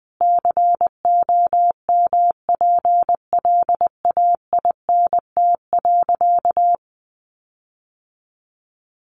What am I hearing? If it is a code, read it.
Morse code, heard as a keyed tone, about 20 wpm, 700 Hz. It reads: COMPLAINT.
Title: Sentences from Top 1400 Words 20wpm